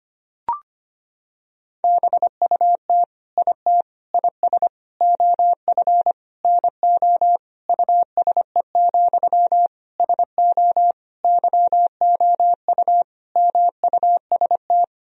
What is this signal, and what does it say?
Morse code, heard as a keyed tone, about 25 wpm, 700 Hz. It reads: BUT IT IS OF NO USE, SO YOU MUST
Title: Sentences from Top 1000 Words - Review Part 2 - 25wpm